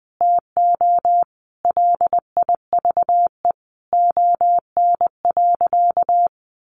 Morse code, heard as a keyed tone, about 20 wpm, 700 Hz. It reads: TO LIVE ON.
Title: Sentences from Top 800 Words 20wpm